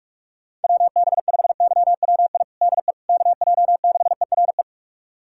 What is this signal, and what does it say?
Morse code, heard as a keyed tone, about 46 wpm, 700 Hz. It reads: WB5YWI DE KJ6ERE
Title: USA Callsign DE Callsign - 6 Characters - 45wpm